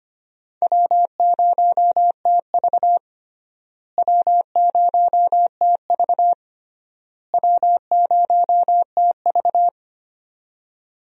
Morse code, heard as a keyed tone, about 25 wpm, 700 Hz. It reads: W0TV W0TV W0TV
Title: USA Call Signs - 4 Characters - Repeated 3x - 25wpm